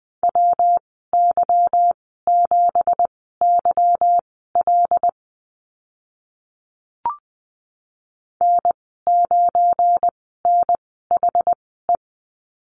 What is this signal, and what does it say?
Morse code, heard as a keyed tone, about 20 wpm, spaced as for 15 wpm, 700 Hz. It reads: WY7YL N9NHE
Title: 173 Lesson 28g - Number 9 - Call Signs 20wpm